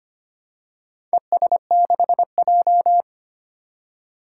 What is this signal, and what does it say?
Morse code, heard as a keyed tone, about 25 wpm, 700 Hz. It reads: ES6J
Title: Four Challenging Letter-Number Combinations 25wpm